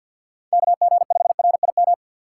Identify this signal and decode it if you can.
Morse code, heard as a keyed tone, about 52 wpm, 700 Hz. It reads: KG5RIM
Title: USA Call Signs - 3x Character Spacing - 50wpm